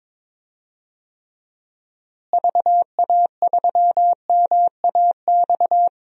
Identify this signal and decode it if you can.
Morse code, heard as a keyed tone, about 22 wpm, 700 Hz. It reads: VA3MAX
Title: DXCC-100 Call Signs - 6 Characters - Repeated 2x - 22wpm